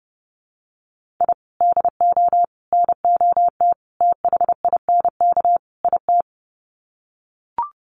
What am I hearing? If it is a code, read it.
Morse code, heard as a keyed tone, about 30 wpm, 700 Hz. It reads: I DO NOT THINK IT
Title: Sets of 5 Words 30wpm